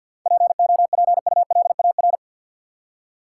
Morse code, heard as a keyed tone, about 48 wpm, 700 Hz. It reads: POPULAR
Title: Seven Letter Words - 50wpm